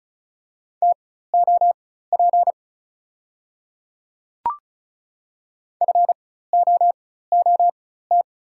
Morse code, heard as a keyed tone, about 36 wpm, spaced as for 16 wpm, 690 Hz. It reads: TOP FOOT
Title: 4x Character Spacing - Top 500 Words - 35wpm